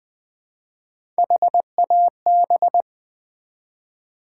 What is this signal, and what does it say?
Morse code, heard as a keyed tone, about 20 wpm, 700 Hz. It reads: HAB
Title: Build A Word - 1 Letter At A Time - 5-Letter Words - 20wpm